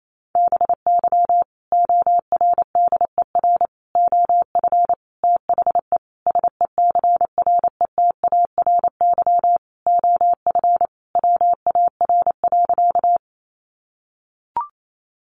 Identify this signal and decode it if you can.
Morse code, heard as a keyed tone, about 28 wpm, 700 Hz. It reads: BY ORDER OF THE SECRETARY OF WAR.
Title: Sentences from Top 1300 Words 28wpm